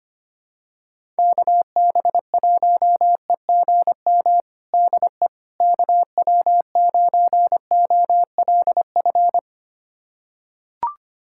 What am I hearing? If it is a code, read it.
Morse code, heard as a keyed tone, about 25 wpm, 700 Hz. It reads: KB1EGM DE KW9OLF
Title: USA Callsign DE Callsign 25wpm